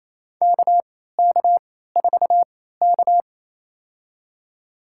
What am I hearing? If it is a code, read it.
Morse code, heard as a keyed tone, about 28 wpm, spaced as for 16 wpm, 700 Hz. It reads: KK4K
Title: USA Call Signs - 4 Characters - 3x Character Spacing - 28wpm